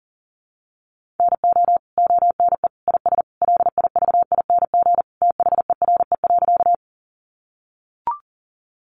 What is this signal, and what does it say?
Morse code, heard as a keyed tone, about 37 wpm, 700 Hz. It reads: NO ONE IS LIVING THERE.
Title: Sentences from Top 1100 Words 40wpm